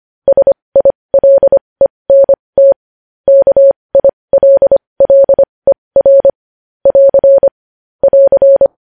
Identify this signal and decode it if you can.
Morse code, heard as a keyed tone, about 25 wpm, spaced as for 20 wpm, 550 Hz. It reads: SILENT KILLER <AR> <AR>
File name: podcast / News headlines-1x-25x20